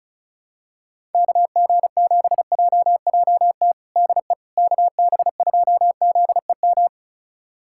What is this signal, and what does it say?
Morse code, heard as a keyed tone, about 34 wpm, 700 Hz. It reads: KG7JJT DE KB2ZEM
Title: USA Callsign DE Callsign - 6 Characters - 35wpm